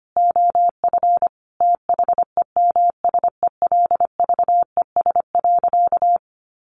Morse code, heard as a keyed tone, about 25 wpm, 700 Hz. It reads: OF THEMSELVES.
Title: Sentences from Top 1300 Words 25wpm